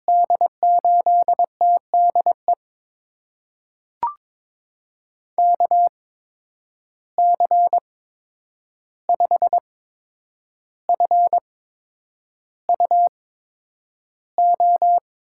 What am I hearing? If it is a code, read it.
Morse code, heard as a keyed tone, about 22 wpm, spaced as for 9 wpm, 700 Hz. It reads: D8TDE KC5FUO
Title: USA Call Signs - 6 Characters - 8x Character Spacing - 22wpm